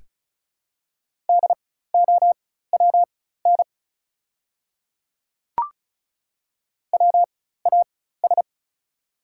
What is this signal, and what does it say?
Morse code, heard as a keyed tone, about 34 wpm, spaced as for 16 wpm, 690 Hz. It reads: DOWN WAS
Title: 4x Character Spacing - Top 500 Words - 35wpm